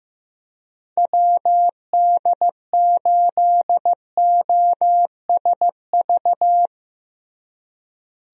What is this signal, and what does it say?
Morse code, heard as a keyed tone, about 15 wpm, 700 Hz. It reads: WD8OSV
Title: USA Call Signs - 6 Characters - Repeated 3x - 15wpm